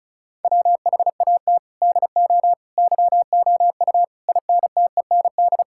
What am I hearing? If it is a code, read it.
Morse code, heard as a keyed tone, about 36 wpm, 700 Hz. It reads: WHAT DO YOU INTEND
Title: Sentences from Top 1600 Words 35wpm